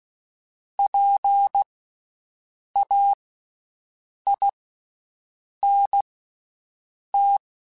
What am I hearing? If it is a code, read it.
Morse code, heard as a keyed tone, about 16 wpm, spaced as for 6 wpm, 800 Hz. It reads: PAINT
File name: Special INGs-mp3s / Special INGs-mp3s / Slow ing @15